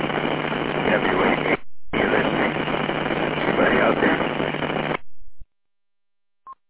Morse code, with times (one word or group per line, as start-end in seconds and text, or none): none